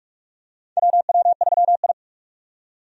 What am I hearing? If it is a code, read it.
Morse code, heard as a keyed tone, about 41 wpm, 700 Hz. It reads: WW3I
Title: USA Call Signs - 4 Characters - 45wpm